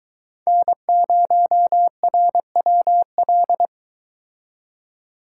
Morse code, heard as a keyed tone, about 23 wpm, 700 Hz. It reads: N0RWL